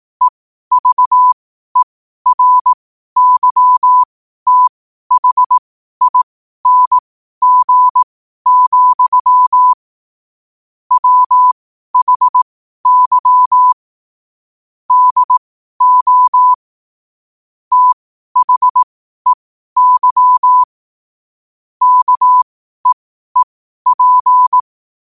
Morse code, heard as a keyed tone, about 18 wpm, spaced as for 12 wpm, 1000 Hz. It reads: EVERYTHING, WHY DO THEY KEEP
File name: qotdfiles / 12wpm